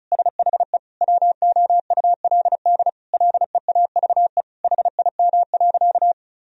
Morse code, heard as a keyed tone, about 34 wpm, 700 Hz. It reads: SHE WOULD LEAVE HIM.
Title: Sentences from Top 1000 Words - Review Part 4 - 35wpm